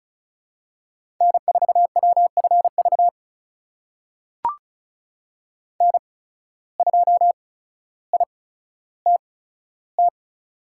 Morse code, heard as a keyed tone, about 36 wpm, spaced as for 15 wpm, 700 Hz. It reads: N4WFV N2ITT
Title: USA Call Signs - 5 Characters - 8x Character Spacing - 35wpm